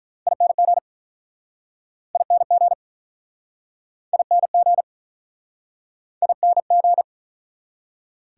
Morse code, heard as a keyed tone, about 49 wpm, 700 Hz. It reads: ING ING ING ING